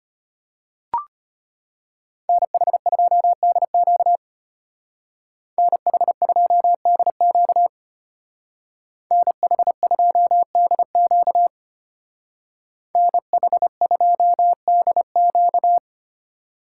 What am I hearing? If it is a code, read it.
Morse code, heard as a keyed tone, about 33 wpm, 700 Hz. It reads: NH2DQ NH2DQ NH2DQ NH2DQ